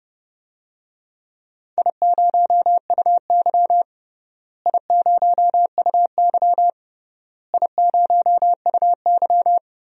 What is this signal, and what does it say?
Morse code, heard as a keyed tone, about 32 wpm, 700 Hz. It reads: I0UY I0UY I0UY
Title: DXCC-100 Call Signs - 4 Characters - Repeated 3x - 30wpm